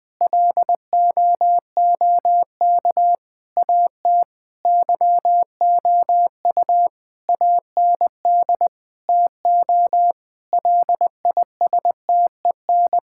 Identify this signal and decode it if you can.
Morse code, heard as a keyed tone, about 21 wpm, 700 Hz. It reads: LOOK AT YOU AND TO LISTEN
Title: Sentences from Top 1600 Words 20wpm